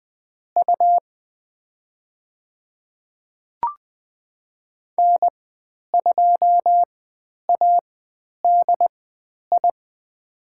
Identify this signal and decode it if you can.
Morse code, heard as a keyed tone, about 20 wpm, spaced as for 10 wpm, 700 Hz. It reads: U N2ADI
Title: 102 Lesson 17g - Letter u - Call Signs 20wpm